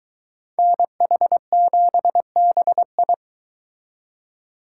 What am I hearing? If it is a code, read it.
Morse code, heard as a keyed tone, about 22 wpm, 700 Hz. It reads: NH7BI